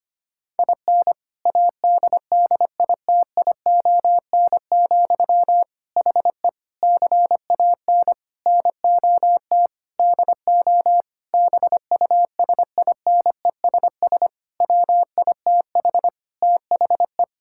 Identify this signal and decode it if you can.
Morse code, heard as a keyed tone, about 25 wpm, 700 Hz. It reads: IN ADDITION, HE CAN NOT DO BUSINESS WITH THE
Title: Sentences from Top 1000 Words - Review Part 6 - 25wpm